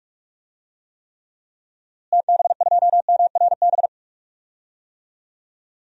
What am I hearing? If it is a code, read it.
Morse code, heard as a keyed tone, about 46 wpm, 700 Hz. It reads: TB2MRB
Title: DXCC-100 Call Signs - 6 Characters - Repeated 2x - 45wpm